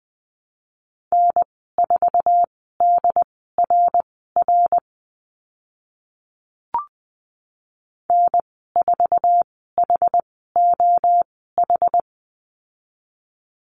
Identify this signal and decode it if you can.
Morse code, heard as a keyed tone, about 20 wpm, spaced as for 14 wpm, 700 Hz. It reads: N4DRR N4HOH
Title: 088 Lesson 15g - Number 5 - Call Signs 20wpm